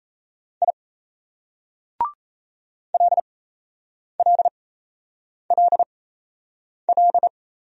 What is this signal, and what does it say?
Morse code, heard as a keyed tone, about 36 wpm, spaced as for 8 wpm, 700 Hz. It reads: I LLLL